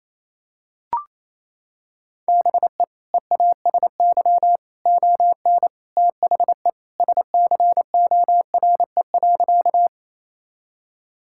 Morse code, heard as a keyed tone, about 28 wpm, 700 Hz. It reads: BE EASY ON THE SCORE.
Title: Sentences from Top 1600 Words 28wpm